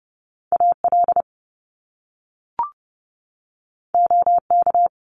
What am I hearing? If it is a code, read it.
Morse code, heard as a keyed tone, about 30 wpm, 700 Hz. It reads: A L O K
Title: Top Two Letter Combinations in Words 30wpm